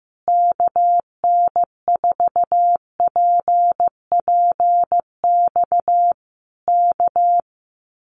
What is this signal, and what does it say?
Morse code, heard as a keyed tone, about 15 wpm, 700 Hz. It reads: KN4PPX K
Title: Common QSO Phrases Encore 15wpm